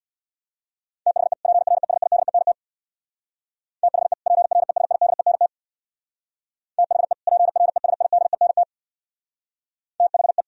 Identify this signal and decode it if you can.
Morse code, heard as a keyed tone, about 65 wpm, 700 Hz. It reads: THE PRESIDENT THE PRESIDENT THE PRESIDENT THE